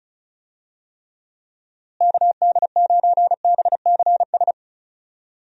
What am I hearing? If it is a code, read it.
Morse code, heard as a keyed tone, about 34 wpm, 700 Hz. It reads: KD9BCS
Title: USA Call Signs - 6 Characters - Repeated 3x - 35wpm